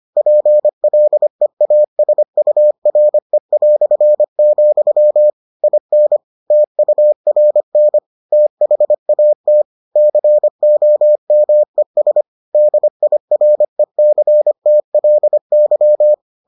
Morse code, heard as a keyed tone, about 25 wpm, 600 Hz. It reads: PLEASURE", IN TURN THAT COMES DIRECTLY